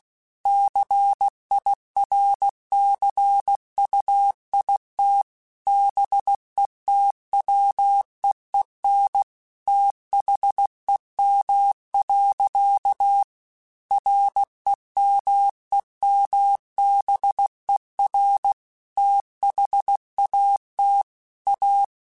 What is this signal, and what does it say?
Morse code, heard as a keyed tone, about 16 wpm, 800 Hz. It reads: CIRCUIT BETWEEN THEM. REMEMBER THAT A
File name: Technical Subjects / Technical Subjects-mp3s / Sw Pwr Sup @15